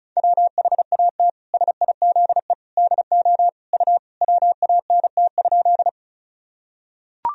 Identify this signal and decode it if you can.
Morse code, heard as a keyed tone, about 36 wpm, 700 Hz. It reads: WHAT SIZE DO U WANT?
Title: Sentences from Top 400 Words and Abbreviations 35wpm